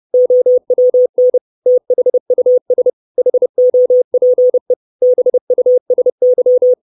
Morse code, heard as a keyed tone, about 30 wpm, 500 Hz. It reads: OWN THUS HOPE BUSY
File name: 632 Words 30WPM